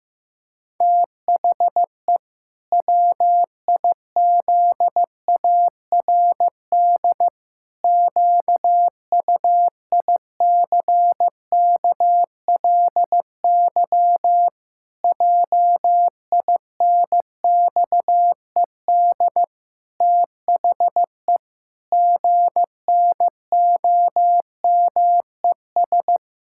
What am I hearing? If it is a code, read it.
Morse code, heard as a keyed tone, about 15 wpm, 700 Hz. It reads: THE WIZARD QUICKLY JINXED THE GNOMES
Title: Pangram Challenge v2 - 15wpm